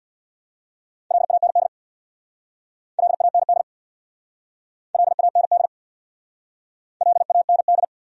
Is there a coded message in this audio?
Yes: Morse code from 1.1 to 7.8 s, heard as a keyed tone, about 66 wpm, 700 Hz. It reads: LAND LAND LAND LAND